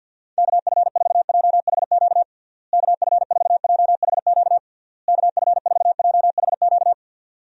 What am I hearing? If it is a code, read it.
Morse code, heard as a keyed tone, about 49 wpm, 700 Hz. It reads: KF4JHQ KF4JHQ KF4JHQ
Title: USA Call Signs - 6 Characters - Repeated 3x - 50wpm